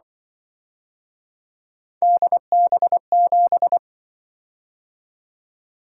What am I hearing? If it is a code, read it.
Morse code, heard as a keyed tone, about 24 wpm, 700 Hz. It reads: DB7
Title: Speed-Racing - Three Letter-Number Combinations - 20wpm